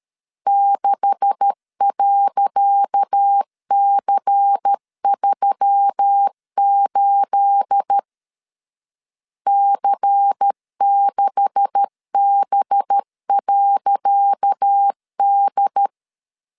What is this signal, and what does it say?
Morse code, heard as a keyed tone, about 13 wpm, 800 Hz. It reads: 6.C38 C6B.D
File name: LEZIONI / 10 SIGNS / 10C DOT 12wpm 2451